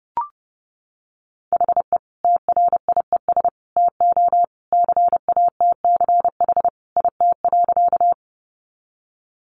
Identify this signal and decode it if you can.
Morse code, heard as a keyed tone, about 30 wpm, 700 Hz. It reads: HE TRIES TO CATCH IT.
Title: Additional Easy Phrases 30wpm